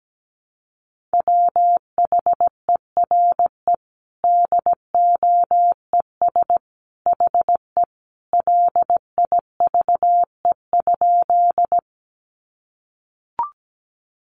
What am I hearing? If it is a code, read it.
Morse code, heard as a keyed tone, about 17 wpm, 700 Hz. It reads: WHERE DOES HE LIVE?
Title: Sentences from Top 200 Words 17wpm